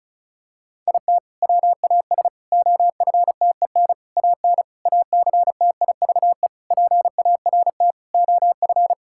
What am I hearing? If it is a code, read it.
Morse code, heard as a keyed tone, about 36 wpm, 700 Hz. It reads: IT WAS OFTEN AN ACTIVE PART OF
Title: Sentences from Top 1000 Words - Review Part 6 - 35wpm